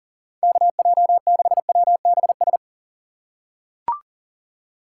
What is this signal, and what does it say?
Morse code, heard as a keyed tone, about 40 wpm, 700 Hz. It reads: KJ6WBS
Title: USA Call Signs - 6 Characters - 6x Character Spacing - 40wpm